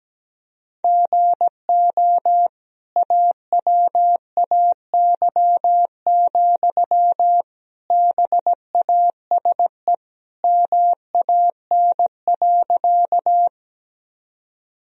Morse code, heard as a keyed tone, about 17 wpm, 700 Hz. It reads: GO AWAY, BASE MAN.
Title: Sentences from Top 1000 Words - Review Part 1 - 17wpm